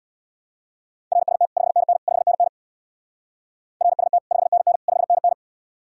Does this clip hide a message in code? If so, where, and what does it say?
Yes: Morse code from 1.1 to 5.3 s, heard as a keyed tone, about 65 wpm, 700 Hz. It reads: RST 5NN 5NN RST 5NN 5NN